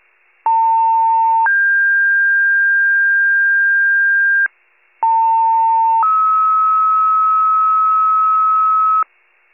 Two-tone page, heard as a one-second tone, then a three-second tone, about 900 then 1600 Hz, about 900 then 1290 Hz, with noise. It is heard from 0.5 to 9.0 s.